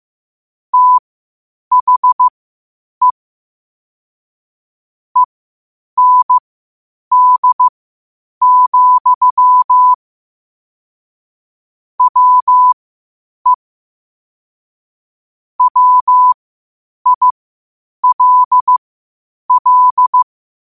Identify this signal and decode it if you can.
Morse code, heard as a keyed tone, about 15 wpm, spaced as for 8 wpm, 1000 Hz. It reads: THE END, WE WILL
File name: qotdfiles / 8wpm